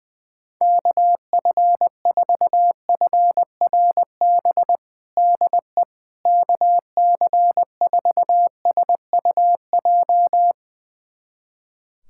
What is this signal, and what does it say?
Morse code, heard as a keyed tone, about 20 wpm, 700 Hz. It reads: KF4FRB DE KC4SUJ
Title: USA Callsign DE Callsign - 6 Characters - 20wpm